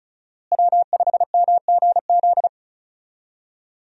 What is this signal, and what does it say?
Morse code, heard as a keyed tone, about 34 wpm, 700 Hz. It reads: W5MGZ
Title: USA Call Signs - 5 Characters - Repeated 2x - 35wpm